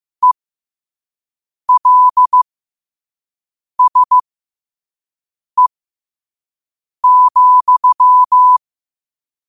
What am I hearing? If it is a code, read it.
Morse code, heard as a keyed tone, about 15 wpm, spaced as for 5 wpm, 1000 Hz. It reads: ELSE,
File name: qotdfiles / 5wpm